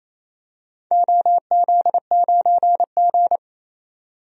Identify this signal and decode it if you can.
Morse code, heard as a keyed tone, about 28 wpm, 700 Hz. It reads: OZ9G
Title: DXCC-100 Call Signs - 4 Characters - 28wpm